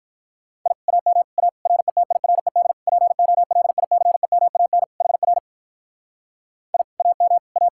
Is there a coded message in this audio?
Yes: Morse code from 0.7 to 7.7 s, heard as a keyed tone, about 46 wpm, 700 Hz. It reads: I AM A RETIRED POLICEMAN HR I AM A